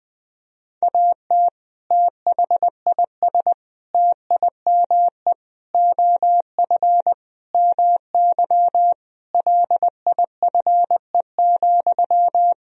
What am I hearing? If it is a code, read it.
Morse code, heard as a keyed tone, about 20 wpm, 700 Hz. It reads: AT THIS TIME OF MY LIFE,
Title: Sentences from Top 300 Words and Abbreviations 20wpm